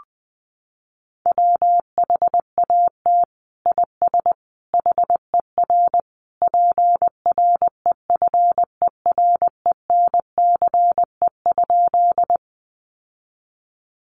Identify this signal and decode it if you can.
Morse code, heard as a keyed tone, about 20 wpm, 700 Hz. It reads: WHAT IS HER PREFERENCE?
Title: Sentences from Top 1500 Words 20wpm